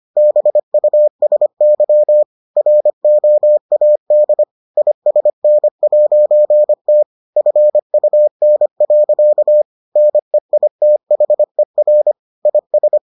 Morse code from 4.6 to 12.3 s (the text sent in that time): ISN'T FUN. NEITHER